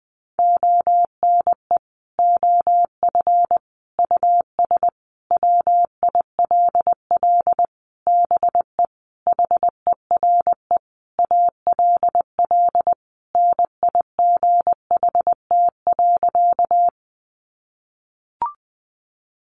Morse code, heard as a keyed tone, about 20 wpm, 700 Hz. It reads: ONE OF US WILL BE HERE ALL NIGHT.
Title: Sentences from Top 300 Words 20wpm